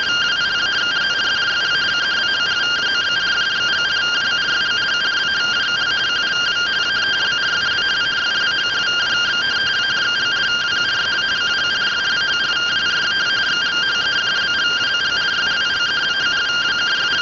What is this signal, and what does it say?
Two-tone teleprinter signal, about 170 Hz shift.
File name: findings / MR CORONAL 55.gen-07 rtty rx